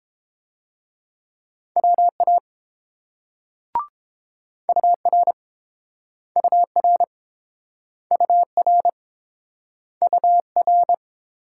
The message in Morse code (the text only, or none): WA UR UR UR UR